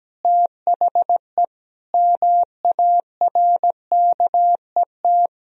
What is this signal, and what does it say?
Morse code, heard as a keyed tone, about 17 wpm, 700 Hz. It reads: THE MARKET
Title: Sentences from Top 1000 Words - Review Part 3 - 17wpm